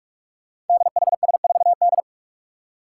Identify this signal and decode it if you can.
Morse code, heard as a keyed tone, about 47 wpm, 700 Hz. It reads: DHS4D
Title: Five Challenging Letter-Number Combinations 45wpm